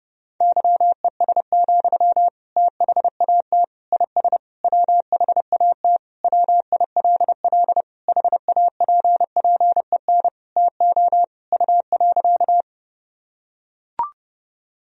Morse code, heard as a keyed tone, about 32 wpm, 700 Hz. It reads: YES, THAT IS WHAT WILL HAPPEN TO U.
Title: Sentences from Top 500 Words and Abbreviations 30wpm